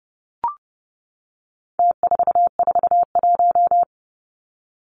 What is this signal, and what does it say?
Morse code, heard as a keyed tone, about 30 wpm, 700 Hz. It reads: T441
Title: Numbers - Contest Style - 4-Digit 30wpm